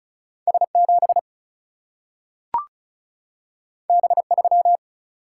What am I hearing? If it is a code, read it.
Morse code, heard as a keyed tone, about 36 wpm, 700 Hz. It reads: S 7 B 3
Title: Single-Letter Single-Number 35wpm